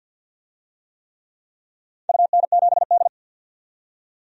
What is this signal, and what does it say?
Morse code, heard as a keyed tone, about 52 wpm, 700 Hz. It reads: UN7D
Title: DXCC-100 Call Signs - 4 Characters - 50wpm